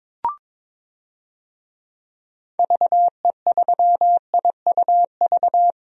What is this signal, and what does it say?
Morse code, heard as a keyed tone, about 22 wpm, 700 Hz. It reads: VE3IUV
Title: DXCC-100 Call Signs - 6 Characters - 22wpm